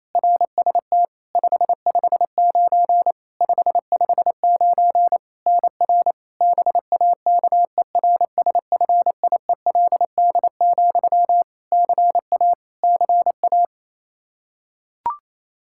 Morse code, heard as a keyed tone, about 28 wpm, 700 Hz. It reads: RST 559 559 NR BAKERSFIELD, CA CA
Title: Practice QSO - Exchange 1 - Beginning - 28wpm